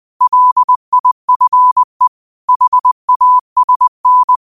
Morse code, heard as a keyed tone, about 20 wpm, 1000 Hz. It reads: LIFE HASN